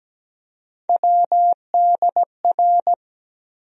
Morse code, heard as a keyed tone, about 17 wpm, 700 Hz. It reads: WDR
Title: USA Call Sign Postfixes - 3 Letters 17wpm